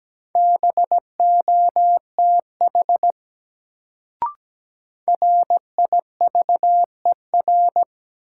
Morse code, heard as a keyed tone, about 17 wpm, 700 Hz. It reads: BOTH RIVER
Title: Sentences from Top 300 Words - New Word Drill - 17wpm